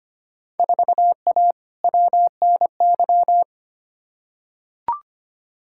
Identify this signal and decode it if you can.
Morse code, heard as a keyed tone, about 25 wpm, 700 Hz. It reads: 4A WNY
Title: Field Day Exchanges Encore - 25wpm